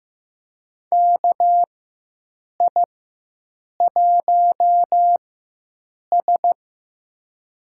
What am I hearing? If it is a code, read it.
Morse code, heard as a keyed tone, about 15 wpm, spaced as for 7 wpm, 700 Hz. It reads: KI1S
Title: USA Call Signs - 4 Characters - 4x Character Spacing - 15wpm